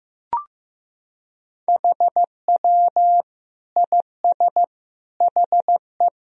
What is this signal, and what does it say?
Morse code, heard as a keyed tone, about 16 wpm, 700 Hz. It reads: HW IS HE
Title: Sentences from Top 100 Words and Abbreviations 15wpm